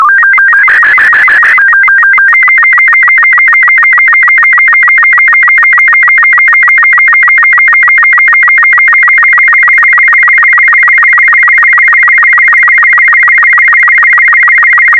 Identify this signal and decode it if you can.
SSTV (slow-scan television), Robot 36.
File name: sstv / R36-VK4MSL